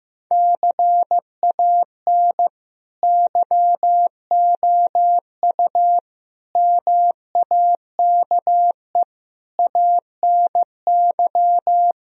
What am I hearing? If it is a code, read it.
Morse code, heard as a keyed tone, about 15 wpm, 700 Hz. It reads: CAN YOU MAKE ANY
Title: Sentences from Top 1000 Words - Review Part 4 - 15wpm